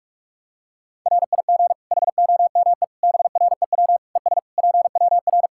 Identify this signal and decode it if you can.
Morse code, heard as a keyed tone, about 41 wpm, 700 Hz. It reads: RIG HOME BREW ES PWR
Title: Practice QSO - Exchange 2 - Info - 45wpm